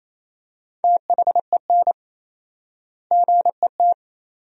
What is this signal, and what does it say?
Morse code, heard as a keyed tone, about 28 wpm, 700 Hz. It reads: THEN GET
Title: 4x Word-Spacing - Sentences from Top 400 Words - 28wpm